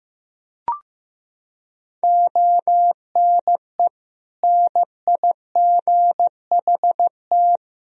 Morse code, heard as a keyed tone, about 15 wpm, 700 Hz. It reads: ONE NIGHT
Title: Sentences from Top 300 Words and Abbreviations 15wpm